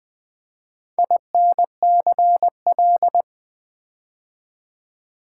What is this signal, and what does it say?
Morse code, heard as a keyed tone, about 20 wpm, 700 Hz. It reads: INCL
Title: Build A Word - 4 Letters At A Time - 8-Letter Words - 20wpm